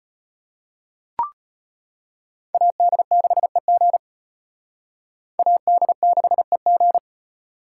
Morse code, heard as a keyed tone, about 31 wpm, 700 Hz. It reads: AD6EG AD6EG
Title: Speed Racing - USA 2x2 Call Signs 25wpm